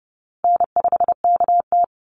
Morse code, heard as a keyed tone, about 28 wpm, 700 Hz. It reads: N5KT
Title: Easy Call Signs Repeated 3x - 30wpm